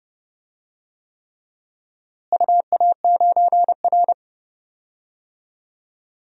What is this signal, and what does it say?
Morse code, heard as a keyed tone, about 32 wpm, 700 Hz. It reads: UA9R
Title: DXCC-100 Call Signs - 4 Characters - Repeated 3x - 30wpm